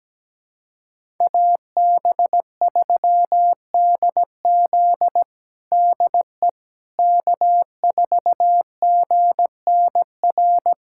Morse code, heard as a keyed tone, about 17 wpm, 700 Hz. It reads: AB3DZ DE K4GNR
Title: USA Callsign DE Callsign - 5 Characters - 17wpm